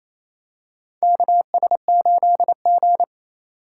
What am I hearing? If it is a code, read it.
Morse code, heard as a keyed tone, about 28 wpm, 700 Hz. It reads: KS8G
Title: USA Call Signs - 4 Characters - 4x Character Spacing - 28wpm